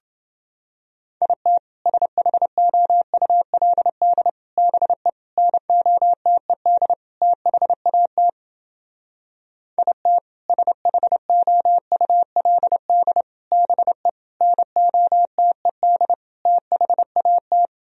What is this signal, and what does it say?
Morse code, heard as a keyed tone, about 28 wpm, 700 Hz. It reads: IT SHOULD BE NOTED THAT IT SHOULD BE NOTED THAT